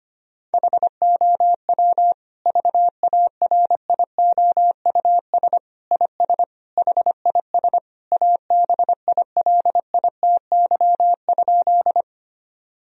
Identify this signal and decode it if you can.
Morse code, heard as a keyed tone, about 25 wpm, 700 Hz. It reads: HOW VARIOUS IS HIS ABILITY?
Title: Sentences from Top 1000 Words - Review Part 4 - 25wpm